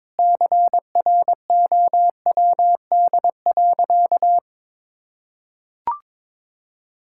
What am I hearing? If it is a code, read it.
Morse code, heard as a keyed tone, about 22 wpm, 700 Hz. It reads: CROWD.
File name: practice / 22wpm / Sentences with 4-Word Phrases - 22wpm